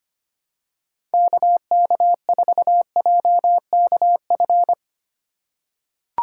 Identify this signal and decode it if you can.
Morse code, heard as a keyed tone, about 25 wpm, 700 Hz. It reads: KK4JKF
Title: USA Call Signs - 6 Characters - 3x Character Spacing - 25wpm